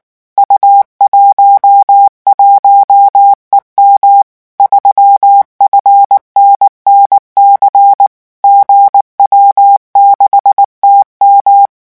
Morse code, heard as a keyed tone, about 19 wpm, 800 Hz. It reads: U11EM 3FNNC GW6TM
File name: morse mp3 / revision10-vitesse 18 mots